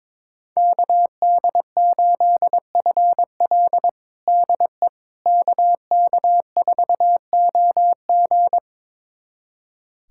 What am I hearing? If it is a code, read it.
Morse code, heard as a keyed tone, about 22 wpm, 700 Hz. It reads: KD8FL DE KK4OG
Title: USA Callsign DE Callsign - 5 Characters - 22wpm